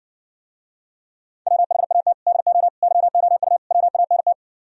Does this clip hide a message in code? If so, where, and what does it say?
Yes: Morse code from 1.5 to 4.3 s, heard as a keyed tone, about 69 wpm, 700 Hz. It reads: WHAT DO YOU WANT